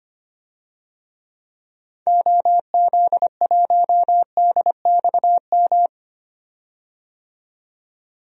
Morse code, heard as a keyed tone, about 25 wpm, 700 Hz. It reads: OZ1DXM
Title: DXCC-100 Call Signs - 6 Characters - 25wpm